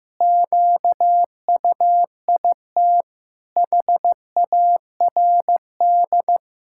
Morse code, heard as a keyed tone, about 16 wpm, 700 Hz. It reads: QUIT HARD
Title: Pangram Challenge v2 - 15wpm